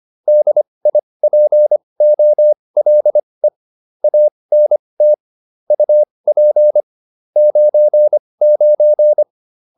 Morse code, heard as a keyed tone, about 25 wpm, spaced as for 20 wpm, 600 Hz. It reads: DIPOLE ANT UP 99
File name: QSO304 20